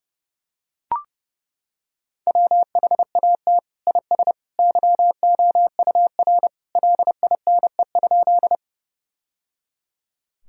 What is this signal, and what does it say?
Morse code, heard as a keyed tone, about 30 wpm, 700 Hz. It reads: WHAT IS YOUR LINE?
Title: Sentences from Top 1000 Words - Review Part 1 - 30wpm